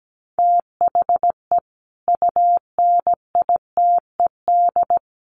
Morse code, heard as a keyed tone, about 17 wpm, 700 Hz. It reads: THE UNITED
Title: Sets of 5 Words Encore 17wpm